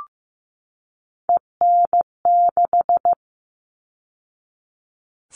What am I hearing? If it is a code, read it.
Morse code, heard as a keyed tone, about 15 wpm, 700 Hz. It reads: EN6
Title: All Cut Numbers - 3-Digit 15wpm